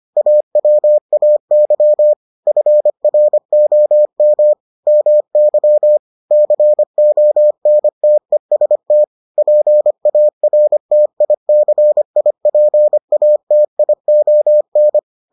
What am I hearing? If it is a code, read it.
Morse code, heard as a keyed tone, about 25 wpm, 600 Hz. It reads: AWAY FROM MY CONTEST PARTICIPATION